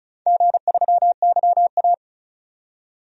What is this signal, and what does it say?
Morse code, heard as a keyed tone, about 36 wpm, 690 Hz. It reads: G3YA
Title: DXCC-100 Call Signs - 4 Characters - 35wpm